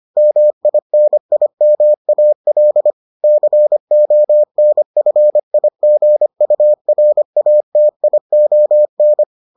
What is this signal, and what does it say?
Morse code, heard as a keyed tone, about 25 wpm, 600 Hz. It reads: MINIMAL CONFIGURATION